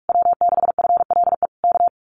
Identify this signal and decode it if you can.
Morse code, heard as a keyed tone, about 46 wpm, 700 Hz. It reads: W6FLE K
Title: Common QSO Phrases Encore 45wpm